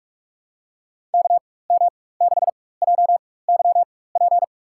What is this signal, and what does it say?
Morse code, heard as a keyed tone, about 41 wpm, spaced as for 20 wpm, 690 Hz. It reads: KM6JYP
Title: USA Call Signs - 4x Character Spacing - 45wpm